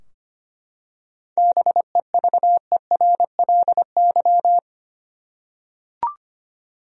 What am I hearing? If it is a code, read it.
Morse code, heard as a keyed tone, about 25 wpm, 700 Hz. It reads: BEVERLY
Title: USA Common Names - 25wpm